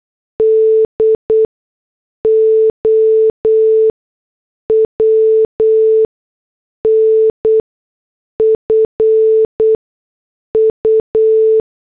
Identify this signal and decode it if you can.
Morse code, heard as a keyed tone, about 8 wpm, spaced as for 6 wpm, 440 Hz. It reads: DOWNFU